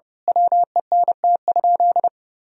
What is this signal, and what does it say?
Morse code, heard as a keyed tone, about 30 wpm, 700 Hz. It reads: WENT?
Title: Sentences from Top 1000 Words - Review Part 3 - 30wpm